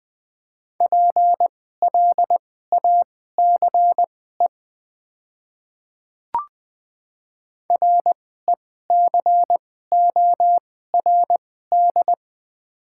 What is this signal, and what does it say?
Morse code, heard as a keyed tone, about 20 wpm, spaced as for 14 wpm, 700 Hz. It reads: PLACE RECORD
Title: 2x Character Spacing - Top 500 Words - 20wpm